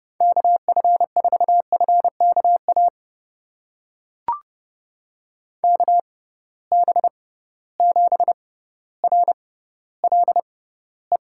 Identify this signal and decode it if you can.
Morse code, heard as a keyed tone, about 30 wpm, spaced as for 15 wpm, 700 Hz. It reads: KF4FKA KB7RLE
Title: USA Call Signs - 6 Characters - 6x Character Spacing - 30wpm